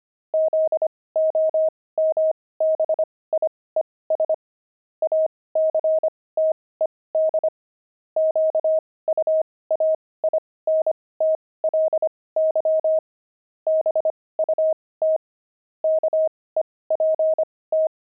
Morse code, heard as a keyed tone, about 25 wpm, spaced as for 18 wpm, 630 Hz. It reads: ZOMBIES ACTED QUAINTLY BUT KEPT
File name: pangram-fs-18-cs-25